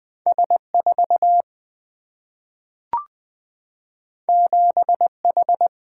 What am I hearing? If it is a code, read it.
Morse code, heard as a keyed tone, about 20 wpm, 700 Hz. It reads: S 4 7 H
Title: Two Letter-Number Combinations - 20wpm